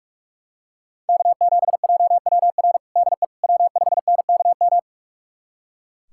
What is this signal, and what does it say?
Morse code, heard as a keyed tone, about 44 wpm, 700 Hz. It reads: K7JWR DE W5NKM
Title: USA Callsign DE Callsign 45wpm